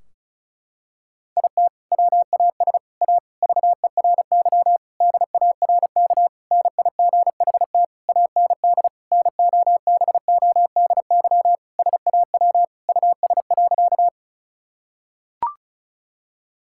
Morse code, heard as a keyed tone, about 36 wpm, 700 Hz. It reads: IT WAS A VERY DARK NIGHT AND NOBODY SAW US.
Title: Sentences from Top 1600 Words 35wpm